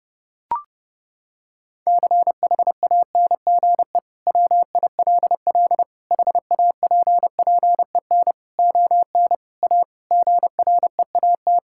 Morse code, heard as a keyed tone, about 30 wpm, 700 Hz. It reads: CHANGE WILL HAPPEN ON A GREAT
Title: Sentences from Top 1000 Words - Review Part 5 - 30wpm